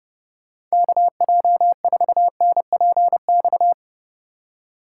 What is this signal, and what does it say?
Morse code, heard as a keyed tone, about 36 wpm, 700 Hz. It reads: KJ4NPX
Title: Speed Racing - USA 2x3 Call Signs 30wpm